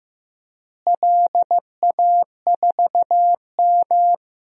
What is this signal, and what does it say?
Morse code, heard as a keyed tone, about 16 wpm, 700 Hz. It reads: LA4M
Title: DXCC-100 Call Signs - 4 Characters - Repeated 3x - 15wpm